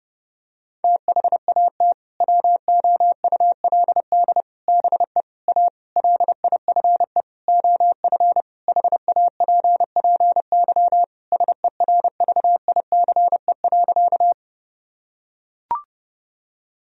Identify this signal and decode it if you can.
Morse code, heard as a keyed tone, about 30 wpm, 700 Hz. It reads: THAT WOULD BE A LIFE OF HAPPY SERVICE.
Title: Sentences from Top 1000 Words - Review Part 4 - 30wpm